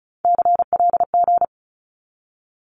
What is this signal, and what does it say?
Morse code, heard as a keyed tone, about 36 wpm, 700 Hz. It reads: CLG
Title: Abbreviations 35wpm